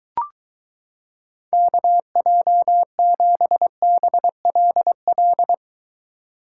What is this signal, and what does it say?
Morse code, heard as a keyed tone, about 22 wpm, 700 Hz. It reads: KJ7BLL